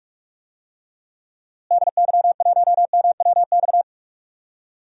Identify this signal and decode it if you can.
Morse code, heard as a keyed tone, about 41 wpm, 700 Hz. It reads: DY1MWX
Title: DXCC-100 Call Signs - 6 Characters - Repeated 3x - 45wpm